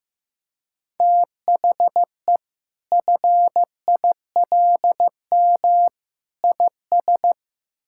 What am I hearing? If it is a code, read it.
Morse code, heard as a keyed tone, about 15 wpm, 700 Hz. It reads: THE FILM IS
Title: Sentences from Top 1000 Words - Review Part 6 - 15wpm